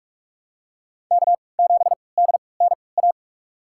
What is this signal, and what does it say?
Morse code, heard as a keyed tone, about 46 wpm, spaced as for 26 wpm, 700 Hz. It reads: K7DNA
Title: USA Call Signs - 5 Characters - 3x Character Spacing - 45wpm